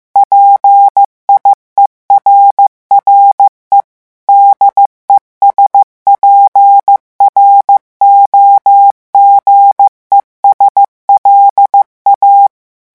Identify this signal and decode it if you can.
Morse code, heard as a keyed tone, about 15 wpm, 800 Hz. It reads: PIERRE DESPROGES LA
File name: morse mp3 / revision9 citations humoristiques vitesse 14 mots minute